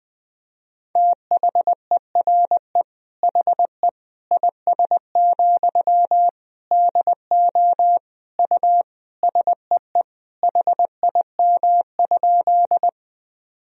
Morse code, heard as a keyed tone, about 20 wpm, 700 Hz. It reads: THERE HE IS, DO U SEE HIM?
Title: Sentences from Top 100 Words and Abbreviations 20wpm